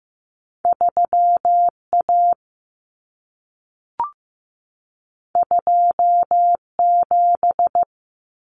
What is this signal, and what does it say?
Morse code, heard as a keyed tone, about 15 wpm, 700 Hz. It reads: 3A 27
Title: Common Cut Numbers - 2-Digit 15wpm